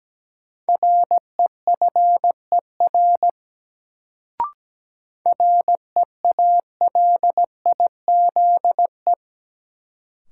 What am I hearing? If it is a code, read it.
Morse code, heard as a keyed tone, about 17 wpm, 700 Hz. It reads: REFER REALIZE
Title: Sentences from Top 1000 Words - New Word Drill - 17wpm